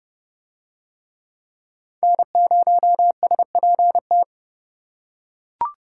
Morse code, heard as a keyed tone, about 30 wpm, 700 Hz. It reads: N0SPT